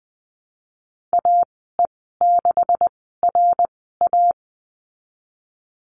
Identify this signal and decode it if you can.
Morse code, heard as a keyed tone, about 20 wpm, spaced as for 15 wpm, 700 Hz. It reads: AE6RA
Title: 179 Lesson 29g - Slash - Call Signs 20wpm